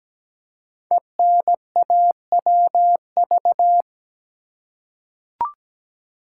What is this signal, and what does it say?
Morse code, heard as a keyed tone, about 17 wpm, 700 Hz. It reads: ENAWV
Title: Code Groups - 5 Letters - 17wpm